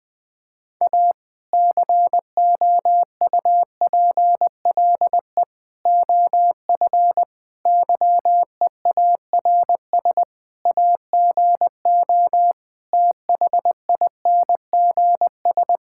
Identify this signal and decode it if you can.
Morse code, heard as a keyed tone, about 20 wpm, 700 Hz. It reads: A COUPLE OF YEARS AGO THINGS
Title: Sentences with 5-Word Phrases - 20wpm